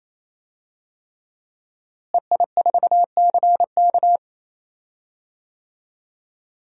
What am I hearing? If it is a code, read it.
Morse code, heard as a keyed tone, about 28 wpm, 700 Hz. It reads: EI4CK
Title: DXCC-100 Call Signs - 5 Characters - Repeated 3x - 28wpm